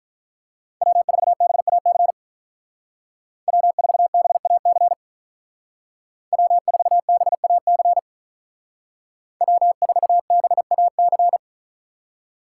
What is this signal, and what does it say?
Morse code, heard as a keyed tone, about 44 wpm, 700 Hz. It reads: W4BAC W4BAC W4BAC W4BAC
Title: Speed Racing - USA Call Signs 35wpm